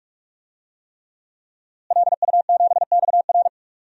Morse code, heard as a keyed tone, about 46 wpm, 700 Hz. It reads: LU7XR
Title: DXCC-100 Call Signs - 5 Characters - Repeated 3x - 45wpm